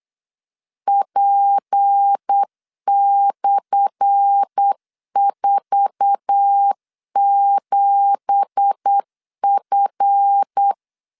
Morse code, heard as a keyed tone, about 8 wpm, 800 Hz. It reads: P/47F
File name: LEZIONI / 10 SIGNS / 10A SLASH 8wpm 3452